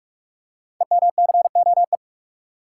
Morse code, heard as a keyed tone, about 44 wpm, 700 Hz. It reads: EMCOE
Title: Code Groups - 5 Letter:Numbers - 45wpm